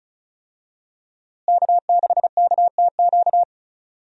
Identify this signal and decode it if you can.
Morse code, heard as a keyed tone, about 36 wpm, 700 Hz. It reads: K6KTQ
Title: USA Call Signs - 5 Characters - Repeated 3x - 35wpm